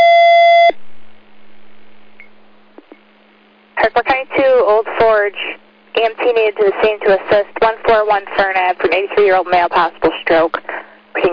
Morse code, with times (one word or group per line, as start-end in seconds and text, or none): none